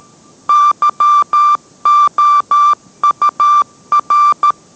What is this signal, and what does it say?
Morse code, heard as a keyed tone, about 13 wpm, 1180 Hz. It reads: YOUR